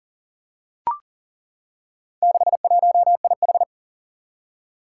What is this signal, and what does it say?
Morse code, heard as a keyed tone, about 43 wpm, 700 Hz. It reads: 61IH
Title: Four Challenging Letter-Number Combinations 40wpm